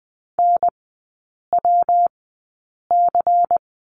Morse code, heard as a keyed tone, about 20 wpm, spaced as for 8 wpm, 700 Hz. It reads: NWC
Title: 152 Lesson 25e - Letter p - Three Characters 20wpm